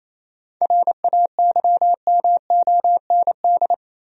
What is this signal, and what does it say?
Morse code, heard as a keyed tone, about 28 wpm, 700 Hz. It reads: RAYMOND
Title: USA Common Names - 28wpm